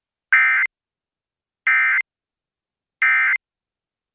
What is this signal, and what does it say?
Emergency Alert System (SAME) tones, heard as harsh buzzing bursts. It is heard from 0.3 to 3.4 s.